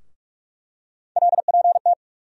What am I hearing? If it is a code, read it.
Morse code, heard as a keyed tone, about 46 wpm, 700 Hz. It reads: LPT
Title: USA Call Sign Postfixes - 3 Letters 45wpm